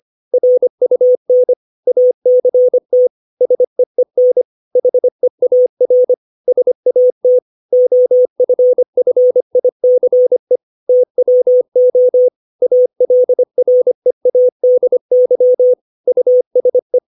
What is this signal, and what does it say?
Morse code, heard as a keyed tone, about 25 wpm, 500 Hz. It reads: RUN ACT SEEN HEAR SAT OFFICE TWO ALREADY USE
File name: 632 words / 632 Words 25WPM